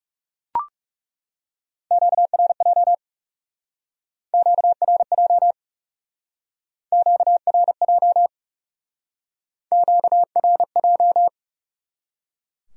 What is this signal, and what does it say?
Morse code, heard as a keyed tone, about 40 wpm, 700 Hz. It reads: QRJ QRJ QRJ QRJ